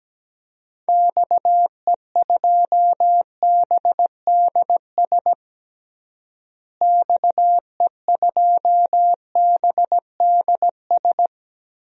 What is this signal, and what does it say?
Morse code, heard as a keyed tone, about 17 wpm, 700 Hz. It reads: XE2BDS XE2BDS
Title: DXCC-100 Call Signs - 6 Characters - Repeated 3x - 17wpm